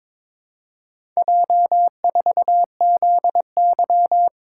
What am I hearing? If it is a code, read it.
Morse code, heard as a keyed tone, about 22 wpm, 700 Hz. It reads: J4ZY
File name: practice / 22wpm / Four Letter-Number Combinations - 22wpm